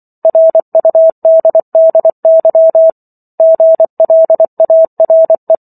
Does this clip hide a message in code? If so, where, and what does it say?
Yes: Morse code from 0.2 to 5.5 s, heard as a keyed tone, about 24 wpm, 650 Hz. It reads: RUDDY GLARE